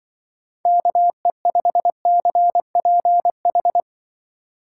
Morse code, heard as a keyed tone, about 25 wpm, 700 Hz. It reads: KE5CPH
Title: Speed Racing - USA 2x3 Call Signs 20wpm